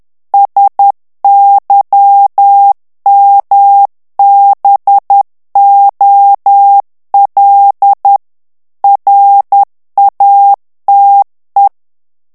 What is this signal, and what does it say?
Morse code, heard as a keyed tone, about 10 wpm, 800 Hz. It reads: SYMBOL RATE
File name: ANGLAIS texte clair 3 vitesse 10 mots minute